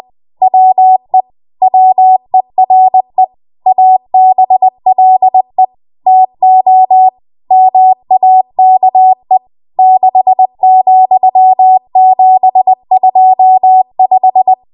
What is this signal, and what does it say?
Morse code, heard as a keyed tone, about 20 wpm, 750 Hz. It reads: WE WERE ABLE TO MAKE 6,725
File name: Morse / 20WPM